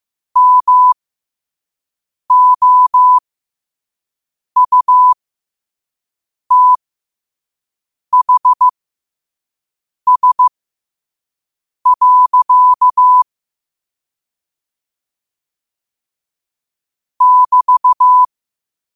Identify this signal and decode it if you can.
Morse code, heard as a keyed tone, about 15 wpm, spaced as for 5 wpm, 1000 Hz. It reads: MOUTHS. <BT>